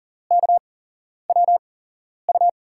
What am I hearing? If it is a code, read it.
Morse code, heard as a keyed tone, about 39 wpm, spaced as for 11 wpm, 700 Hz. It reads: KWU
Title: USA Call Signs - 5 Characters - 8x Character Spacing - 40wpm